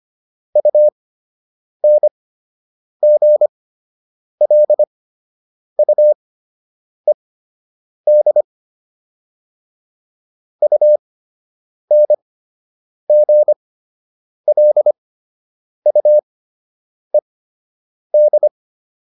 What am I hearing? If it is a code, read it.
Morse code, heard as a keyed tone, about 25 wpm, spaced as for 8 wpm, 600 Hz. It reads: UNGLUED UNGLUED